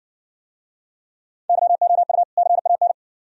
Morse code, heard as a keyed tone, about 57 wpm, 690 Hz. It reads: YOU CAN